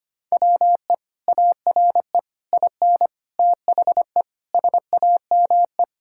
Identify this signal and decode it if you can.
Morse code, heard as a keyed tone, about 25 wpm, 700 Hz. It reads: WE ARE IN THE SAME
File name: practice / 25wpm / Sentences with 3-Word Phrases - 25wpm